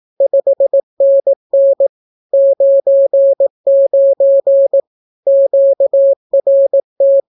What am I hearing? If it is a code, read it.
Morse code, heard as a keyed tone, about 18 wpm, 550 Hz. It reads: 5NN 99 QRT
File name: naukapodstaw / skroty / skroty